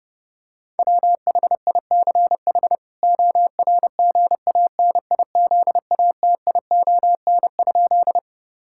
Morse code, heard as a keyed tone, about 30 wpm, 700 Hz. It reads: WHICH ORGANIZATION?
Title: Sentences from Top 1000 Words - Review Part 5 - 30wpm